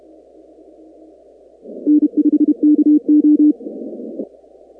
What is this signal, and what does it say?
Morse code, heard as a keyed tone, about 32 wpm, 300 Hz. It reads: N5KO
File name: contests / 2019-cw-wpx-kt5j-k5pi- -2330 0-19